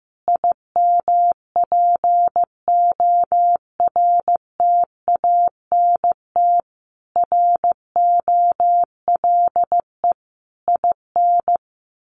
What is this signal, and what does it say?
Morse code, heard as a keyed tone, about 15 wpm, 700 Hz. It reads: IMPORTANT ROLE IN
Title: Sets of 5 Words Encore 15wpm